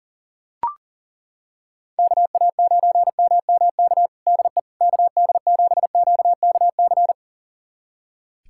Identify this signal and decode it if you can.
Morse code, heard as a keyed tone, about 40 wpm, 700 Hz. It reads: KA9MMK DE KD7QKC
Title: USA Callsign DE Callsign 40wpm